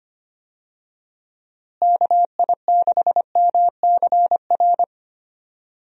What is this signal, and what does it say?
Morse code, heard as a keyed tone, about 25 wpm, 700 Hz. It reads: KI6MCR
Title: USA Call Signs - 6 Characters - Repeated 2x - 25wpm